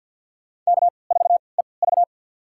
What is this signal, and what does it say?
Morse code, heard as a keyed tone, about 51 wpm, spaced as for 29 wpm, 700 Hz. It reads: K4EV
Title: USA Call Signs - 3x Character Spacing - 50wpm